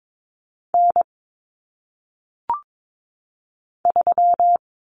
Morse code, heard as a keyed tone, about 22 wpm, 700 Hz. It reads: N 3
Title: Common Cut Numbers - 1-Digit 22wpm